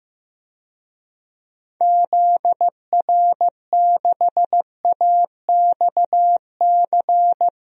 Morse code, heard as a keyed tone, about 15 wpm, 700 Hz. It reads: ZR6AXC
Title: DXCC-100 Call Signs - 6 Characters - 15wpm